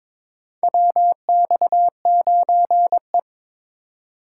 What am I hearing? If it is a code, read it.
Morse code, heard as a keyed tone, about 21 wpm, 700 Hz. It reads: WX9E
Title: USA Call Signs - 4 Characters - 3x Character Spacing - 22wpm